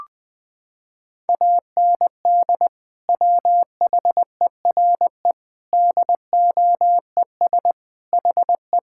Morse code, heard as a keyed tone, about 20 wpm, 700 Hz. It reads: AND WHERE DOES HE
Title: Sentences from Top 1000 Words - Review Part 5 - 20wpm